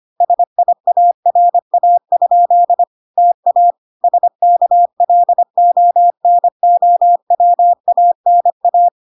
Morse code, heard as a keyed tone, about 25 wpm, 700 Hz. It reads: SIARA? TA SKLONOWANA